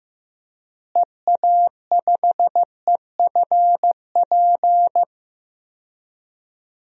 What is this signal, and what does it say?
Morse code, heard as a keyed tone, about 15 wpm, 700 Hz. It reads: EA5EFP
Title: DXCC-100 Call Signs - 6 Characters - Repeated 2x - 15wpm